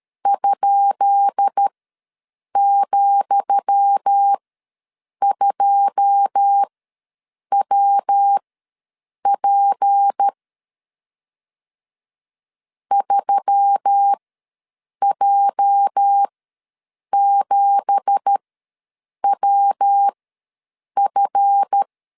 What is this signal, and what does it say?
Morse code, heard as a keyed tone, about 13 wpm, spaced as for 7 wpm, 800 Hz. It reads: ?,2WP 3J7WF
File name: LEZIONI / 10 SIGNS / 10D COMA ws 12wpm 3618